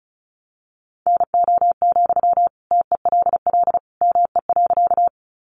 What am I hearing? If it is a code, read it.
Morse code, heard as a keyed tone, about 34 wpm, 700 Hz. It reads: NO, TELL ME.
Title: Sentences from Top 200 Words 35wpm